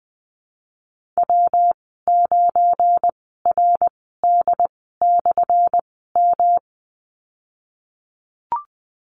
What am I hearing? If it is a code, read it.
Morse code, heard as a keyed tone, about 20 wpm, spaced as for 15 wpm, 700 Hz. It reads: W9RD/M
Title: 179 Lesson 29g - Slash - Call Signs 20wpm